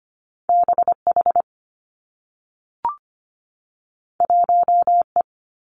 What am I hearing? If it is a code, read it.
Morse code, heard as a keyed tone, about 25 wpm, 700 Hz. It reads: B H 1 E
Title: Two Challenging Letter-Number Combinations 25wpm